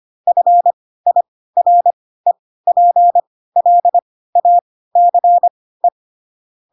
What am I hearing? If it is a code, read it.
Morse code, heard as a keyed tone, about 25 wpm, spaced as for 16 wpm, 700 Hz. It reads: FIREPLACE
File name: Pinocchio / Pinocchio ch3 10 25 ik1hge